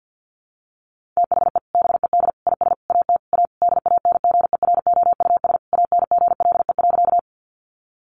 Morse code, heard as a keyed tone, about 50 wpm, 700 Hz. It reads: THE BED IS AT A DANGEROUS ANGLE.
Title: Sentences from Top 1500 Words 50wpm